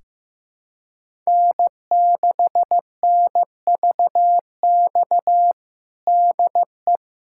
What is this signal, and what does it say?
Morse code, heard as a keyed tone, about 16 wpm, 700 Hz. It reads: N6NVX DE
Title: USA Callsign DE Callsign - 5 Characters - 15wpm